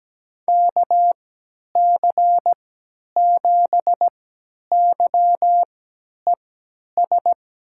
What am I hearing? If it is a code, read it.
Morse code, heard as a keyed tone, about 17 wpm, spaced as for 10 wpm, 700 Hz. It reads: KC7YES
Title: USA Call Signs - 6 Characters - 3x Character Spacing - 17wpm